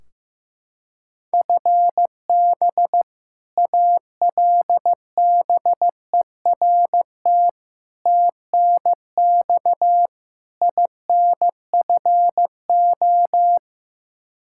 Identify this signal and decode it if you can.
Morse code, heard as a keyed tone, about 15 wpm, 700 Hz. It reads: FB ALBERT TNX INFO
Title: Practice QSO - Exchange 3 - More Info - 15wpm